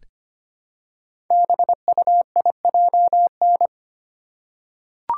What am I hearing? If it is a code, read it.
Morse code, heard as a keyed tone, about 25 wpm, 690 Hz. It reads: BUIJN
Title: Code Groups - 5 Letters - 25wpm